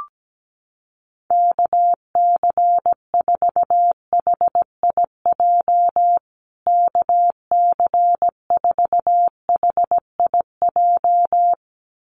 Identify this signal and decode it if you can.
Morse code, heard as a keyed tone, about 17 wpm, 700 Hz. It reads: KC4HIJ KC4HIJ
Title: USA Call Signs Repeated 2x 17wpm